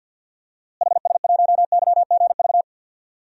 Morse code, heard as a keyed tone, about 53 wpm, 700 Hz. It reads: HS1YGV
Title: DXCC-100 Call Signs - 6 Characters - Repeated 2x - 50wpm